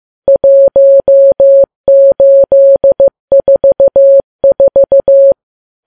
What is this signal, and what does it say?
Morse code, heard as a keyed tone, about 15 wpm, 550 Hz. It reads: 1844
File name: Files Web tls / Morse on9fv / morse15kps